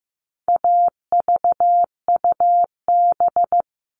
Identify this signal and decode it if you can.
Morse code, heard as a keyed tone, about 15 wpm, 700 Hz. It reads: AVUB
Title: All Cut Numbers - 4-Digit 15wpm